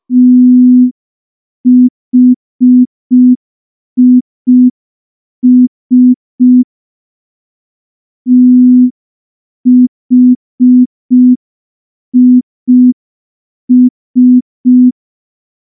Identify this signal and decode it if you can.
Morse code, heard as a keyed tone, about 5 wpm, 250 Hz. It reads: THIS THIS
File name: morse / morse-training-05WPM